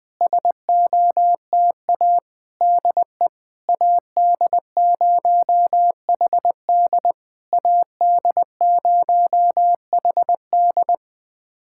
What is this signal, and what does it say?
Morse code, heard as a keyed tone, about 21 wpm, 700 Hz. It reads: SOTA DE AD0HD AD0HD
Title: SOTA QSO Practice 20wpm